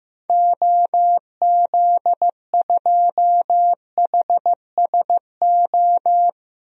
Morse code, heard as a keyed tone, about 15 wpm, 700 Hz. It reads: OZ2HSO
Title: DXCC-100 Call Signs - 6 Characters - 15wpm